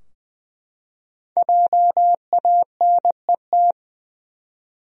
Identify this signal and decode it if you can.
Morse code, heard as a keyed tone, about 20 wpm, 700 Hz. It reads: JANET
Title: USA Common Names - Max 5 Letters - 20wpm